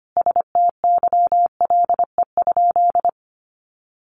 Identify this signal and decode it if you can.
Morse code, heard as a keyed tone, about 25 wpm, 700 Hz. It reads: STYLE?
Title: Sentences from Top 800 Words 25wpm